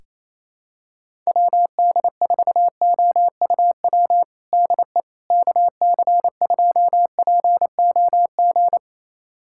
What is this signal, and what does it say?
Morse code, heard as a keyed tone, about 28 wpm, 700 Hz. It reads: WD4OUW DE KC2POG
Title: USA Callsign DE Callsign 28wpm